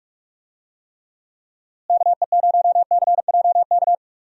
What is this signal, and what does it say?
Morse code, heard as a keyed tone, about 46 wpm, 700 Hz. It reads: KE0CJK
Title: USA Call Signs - 6 Characters - Repeated 2x - 45wpm